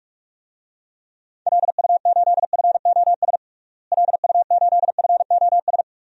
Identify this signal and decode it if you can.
Morse code, heard as a keyed tone, about 41 wpm, 700 Hz. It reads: LU8FOS LU8FOS
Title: DXCC-100 Call Signs - 6 Characters - Repeated 3x - 45wpm